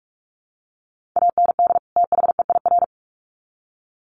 Morse code, heard as a keyed tone, about 44 wpm, 700 Hz. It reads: AND THEIR
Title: Sets of 2 Words 45wpm